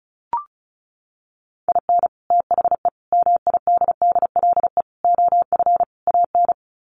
Morse code, heard as a keyed tone, about 34 wpm, 700 Hz. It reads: IN THE MIDDLE OF AN
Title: Sets of 5 Words Encore2 35wpm